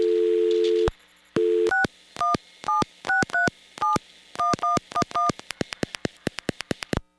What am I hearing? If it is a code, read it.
DTMF, heard as telephone touch tones, tones of about 150 ms. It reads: 6176371111
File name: blackratchet / recordings / charles-st-payphone